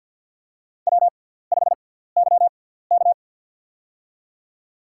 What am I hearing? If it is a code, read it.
Morse code, heard as a keyed tone, about 50 wpm, spaced as for 17 wpm, 690 Hz. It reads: W5YK
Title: USA Call Signs - 4 Characters - 6x Character Spacing - 50wpm